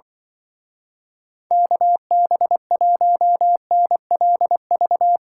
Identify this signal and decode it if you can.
Morse code, heard as a keyed tone, about 24 wpm, 700 Hz. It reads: KB1NLV
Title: Speed Racing - USA Call Signs 20wpm